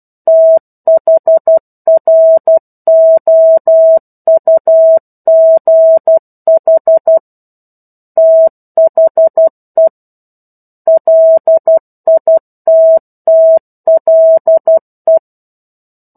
Morse code, heard as a keyed tone, about 12 wpm, 650 Hz. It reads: THROUGH THE LITTLE
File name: 01-Chapter-01 12 WPM